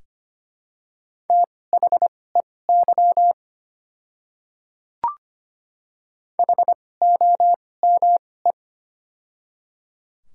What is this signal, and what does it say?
Morse code, heard as a keyed tone, about 25 wpm, spaced as for 18 wpm, 700 Hz. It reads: THEY HOME
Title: 2x Character Spacing - Top 200 Words - 25wpm